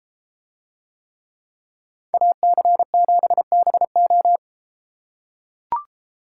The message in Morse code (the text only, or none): AC7BO